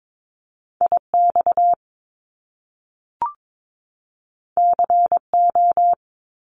Morse code, heard as a keyed tone, about 22 wpm, 700 Hz. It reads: IX CO
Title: Top Two Letter Combinations in Words 22wpm